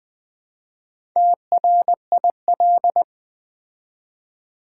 Morse code, heard as a keyed tone, about 20 wpm, 700 Hz. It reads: TRIL
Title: Build A Word - 4 Letters At A Time - 8-Letter Words - 20wpm